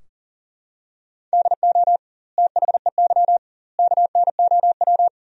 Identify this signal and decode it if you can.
Morse code, heard as a keyed tone, about 40 wpm, 700 Hz. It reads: DO THEY KNOW
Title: 2x Word-Spacing - Sentences from Top 100 Words - 40wpm